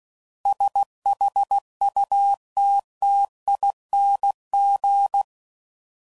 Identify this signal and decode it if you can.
Morse code, heard as a keyed tone, about 16 wpm, 800 Hz. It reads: SHUTTING
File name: Suffixes / Special INGs / Sngl Ings